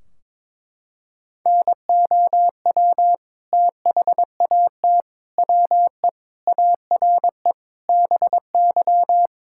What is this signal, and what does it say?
Morse code, heard as a keyed tone, about 22 wpm, 700 Hz. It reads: NOW THAT WE ARE BY
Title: Sentences from Top 1600 Words 22wpm